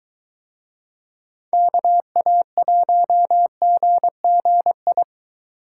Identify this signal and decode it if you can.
Morse code, heard as a keyed tone, about 21 wpm, 700 Hz. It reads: KA1GGI